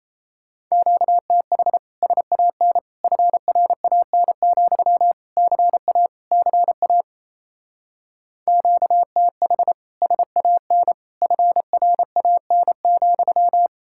Morse code, heard as a keyed tone, about 34 wpm, 700 Hz. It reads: QTH SAN FRAN, CA CA QTH SAN FRAN,
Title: Speed-Racing Common QSO Phrases - 28wpm